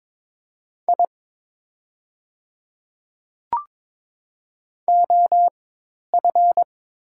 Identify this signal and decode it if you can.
Morse code, heard as a keyed tone, about 22 wpm, spaced as for 10 wpm, 700 Hz. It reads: I OF
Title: 4x Character Spacing - Top 300 Words - 22wpm